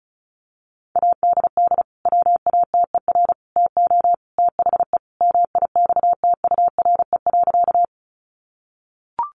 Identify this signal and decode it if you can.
Morse code, heard as a keyed tone, about 34 wpm, 700 Hz. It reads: ADD WATER TO THE MIXTURE.
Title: Sentences from Top 1500 Words 35wpm